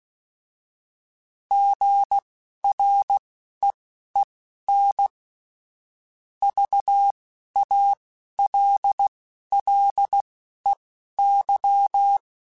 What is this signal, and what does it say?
Morse code, heard as a keyed tone, about 16 wpm, spaced as for 11 wpm, 800 Hz. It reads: GREEN VALLEY